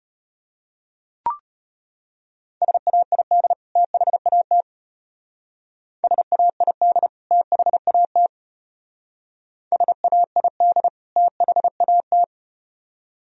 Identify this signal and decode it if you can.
Morse code, heard as a keyed tone, about 32 wpm, 700 Hz. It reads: SAID THAT SAID THAT SAID THAT